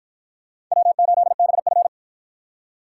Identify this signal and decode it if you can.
Morse code, heard as a keyed tone, about 52 wpm, 700 Hz. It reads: W8BF